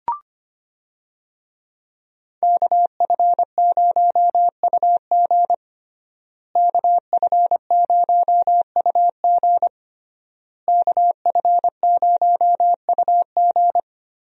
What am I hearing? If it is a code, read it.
Morse code, heard as a keyed tone, about 25 wpm, 700 Hz. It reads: KF0UG KF0UG KF0UG
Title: USA Call Signs - 5 Characters - Repeated 3x - 25wpm